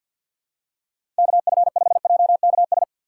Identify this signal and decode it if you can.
Morse code, heard as a keyed tone, about 53 wpm, 700 Hz. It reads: KF5JKS
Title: USA Call Signs - 4x Character Spacing - 50wpm